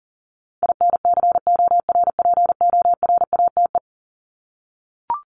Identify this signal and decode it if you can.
Morse code, heard as a keyed tone, about 40 wpm, 700 Hz. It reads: INCORPORATE
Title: Top 1751-2000 Words 40wpm